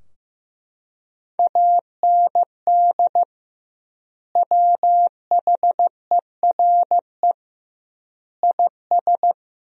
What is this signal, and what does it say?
Morse code, heard as a keyed tone, about 15 wpm, 700 Hz. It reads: AND WHERE IS
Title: 2x Word-Spacing - Sentences from Top 200 Words - 15wpm